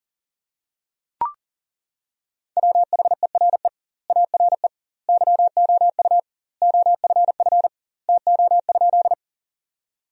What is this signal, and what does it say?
Morse code, heard as a keyed tone, about 40 wpm, 700 Hz. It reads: WHERE ARE YOU OFF TO?
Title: 2x Word-Spacing - Sentences from Top 200 Words - 40wpm